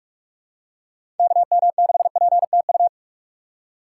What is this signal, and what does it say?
Morse code, heard as a keyed tone, about 42 wpm, 700 Hz. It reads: KM6PTU
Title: USA Call Signs - 6 Characters - 2x Character Spacing - 45wpm